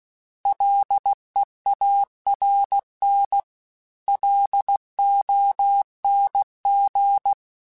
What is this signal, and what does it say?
Morse code, heard as a keyed tone, about 16 wpm, 800 Hz. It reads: LEARN LONG